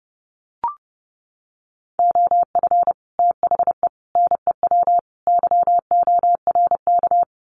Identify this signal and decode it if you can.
Morse code, heard as a keyed tone, about 30 wpm, 700 Hz. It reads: OF THE NEW YORK
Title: Sets of 4 Words 30wpm